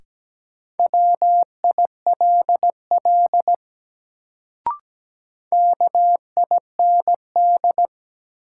Morse code, heard as a keyed tone, about 17 wpm, 700 Hz. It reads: WILL KIND
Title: Sets of 2 Words - New Word Drill - 17wpm